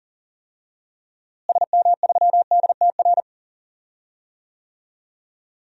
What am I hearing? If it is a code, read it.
Morse code, heard as a keyed tone, about 40 wpm, 700 Hz. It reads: SM3DTR
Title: DXCC-100 Call Signs - 6 Characters - Repeated 3x - 40wpm